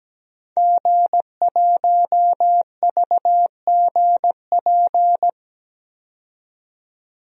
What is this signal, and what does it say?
Morse code, heard as a keyed tone, about 17 wpm, 700 Hz. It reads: G1VGP
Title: DXCC-100 Call Signs - 5 Characters - 17wpm